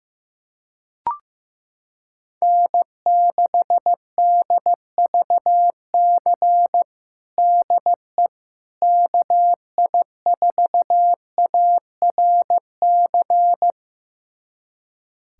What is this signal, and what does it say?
Morse code, heard as a keyed tone, about 15 wpm, 700 Hz. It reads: N6DVC DE KI4ARC
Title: USA Callsign DE Callsign 15wpm